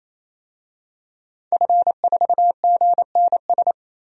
Morse code, heard as a keyed tone, about 28 wpm, 700 Hz. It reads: F4GNS
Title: DXCC-100 Call Signs - 5 Characters - 28wpm